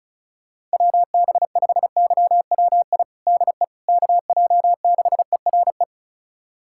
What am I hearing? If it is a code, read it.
Morse code, heard as a keyed tone, about 36 wpm, 700 Hz. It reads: WB5YWI DE KJ6ERE
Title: USA Callsign DE Callsign - 6 Characters - 35wpm